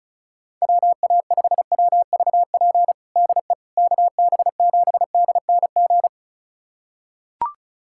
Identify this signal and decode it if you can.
Morse code, heard as a keyed tone, about 34 wpm, 700 Hz. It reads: WA5WVP DE KB7DNG
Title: USA Callsign DE Callsign - 6 Characters - 35wpm